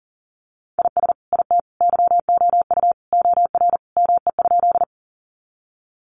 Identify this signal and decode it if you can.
Morse code, heard as a keyed tone, about 40 wpm, 700 Hz. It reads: IS IT YOU OR ME?
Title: Sentences from Top 200 Words 40wpm